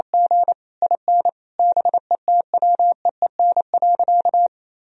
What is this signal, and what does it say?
Morse code, heard as a keyed tone, about 27 wpm, 700 Hz. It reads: G IN BETWEEN.
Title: Sentences from Top 1000 Words - Review Part 1 - 28wpm